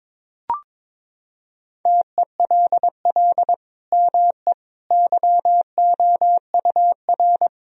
Morse code, heard as a keyed tone, about 22 wpm, 700 Hz. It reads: TELL ME YOUR
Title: Sentences from Top 1000 Words - Review Part 1 - 22wpm